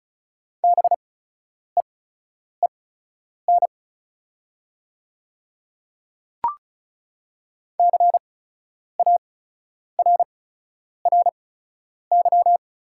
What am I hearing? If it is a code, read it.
Morse code, heard as a keyed tone, about 36 wpm, spaced as for 10 wpm, 690 Hz. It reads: BEEN CARRY
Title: 8x Character Spacing - Top 300 Words - 35wpm